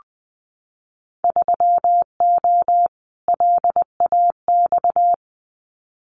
Morse code, heard as a keyed tone, about 20 wpm, 700 Hz. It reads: 3O LAX
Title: Winter Field Day Exchanges Encore 20wpm